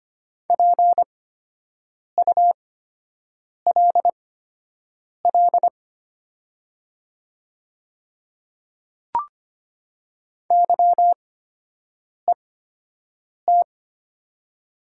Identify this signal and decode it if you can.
Morse code, heard as a keyed tone, about 25 wpm, spaced as for 7 wpm, 700 Hz. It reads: PULL YET
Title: 8x Character Spacing - Top 500 Words - 25wpm